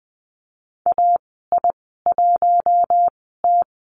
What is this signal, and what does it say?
Morse code, heard as a keyed tone, about 20 wpm, spaced as for 14 wpm, 700 Hz. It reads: AI1T
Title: 042 Lesson 8g - Number 1 - Call Signs 20wpm